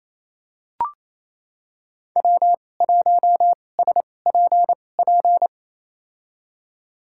Morse code, heard as a keyed tone, about 28 wpm, spaced as for 20 wpm, 700 Hz. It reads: W1SPP
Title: USA Call Signs - 2x Character Spacing - 28wpm